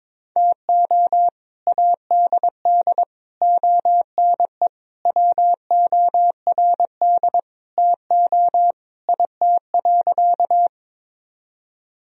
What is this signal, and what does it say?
Morse code, heard as a keyed tone, about 22 wpm, 700 Hz. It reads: TO ADD ONE WORD TO IT.
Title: Sentences from Top 1000 Words - Review Part 1 - 22wpm